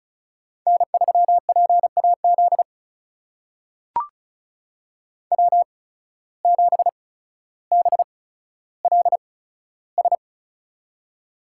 Morse code, heard as a keyed tone, about 36 wpm, spaced as for 15 wpm, 700 Hz. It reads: N3PAZ W7BLS
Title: USA Call Signs - 5 Characters - 8x Character Spacing - 35wpm